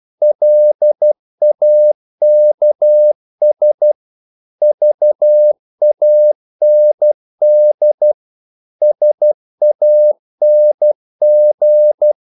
Morse code, heard as a keyed tone, about 12 wpm, 600 Hz. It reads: LAKS VAND SANG
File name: Korte ord DK 12wpm